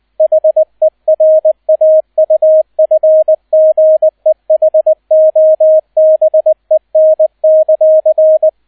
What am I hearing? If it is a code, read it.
Morse code, heard as a keyed tone, about 20 wpm, 630 Hz, with radio noise. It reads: HERAUFGEHOBEN;
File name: 002 Zweiter Streich 20wpm